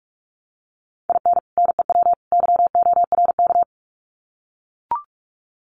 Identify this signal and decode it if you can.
Morse code, heard as a keyed tone, about 44 wpm, 700 Hz. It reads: IN NEW YORK
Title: Sets of 3 Words 45wpm